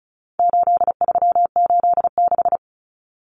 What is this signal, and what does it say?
Morse code, heard as a keyed tone, about 34 wpm, 700 Hz. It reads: 8386
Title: Numbers - 4-Digit 35wpm